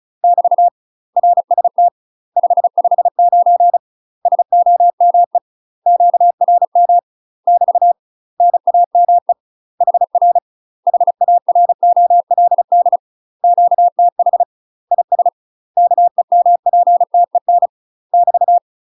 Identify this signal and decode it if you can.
Morse code, heard as a keyed tone, about 34 wpm, 700 Hz. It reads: <BT> RST 559 SOME QRM <BT> NAME HR HAROLD QTH IS KEMPTEN <BT>
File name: UK MASTER QSOs TRAINING / 35WPM / UK MASTER QSOs 0034